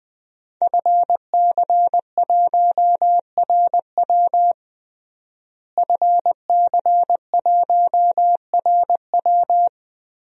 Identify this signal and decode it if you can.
Morse code, heard as a keyed tone, about 21 wpm, 700 Hz. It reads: FC1RW FC1RW
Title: DXCC-100 Call Signs - 5 Characters - Repeated 3x - 20wpm